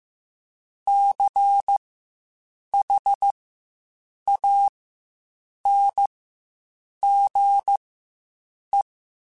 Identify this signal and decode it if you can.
Morse code, heard as a keyed tone, about 15 wpm, spaced as for 7 wpm, 800 Hz. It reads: CHANGE
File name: Quotes Spaced / Quotes Spaced-mp3s / IgnoringFacts sp @10